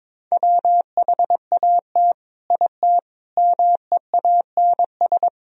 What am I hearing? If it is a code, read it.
Morse code, heard as a keyed tone, about 26 wpm, 700 Hz. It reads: WHAT IT MEANS
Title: Speed-Racing - Sets of 5 Words 22wpm